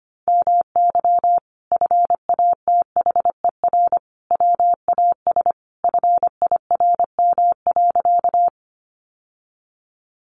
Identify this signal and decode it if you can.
Morse code, heard as a keyed tone, about 25 wpm, 700 Hz. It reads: MY FATHER WAS FIRM.
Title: Sentences from Top 1400 Words 25wpm